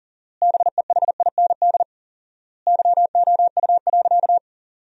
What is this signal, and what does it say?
Morse code, heard as a keyed tone, about 40 wpm, 700 Hz. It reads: BEHIND YOU.
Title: 4x Word-Spacing - Sentences from Top 500 Words - 40wpm